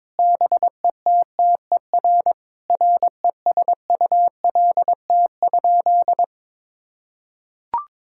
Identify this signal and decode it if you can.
Morse code, heard as a keyed tone, about 22 wpm, 700 Hz. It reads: BETTER RESULT?
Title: Sentences from Top 1000 Words - Review Part 6 - 22wpm